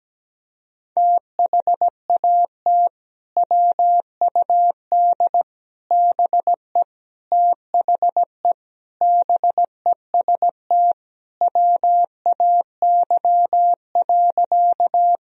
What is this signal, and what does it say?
Morse code, heard as a keyed tone, about 17 wpm, 700 Hz. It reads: THAT WUD BE THE BEST WAY.
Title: Sentences from Top 400 Words and Abbreviations 17wpm